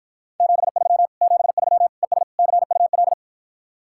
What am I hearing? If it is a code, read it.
Morse code, heard as a keyed tone, about 52 wpm, 700 Hz. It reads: 73 73 ES CUL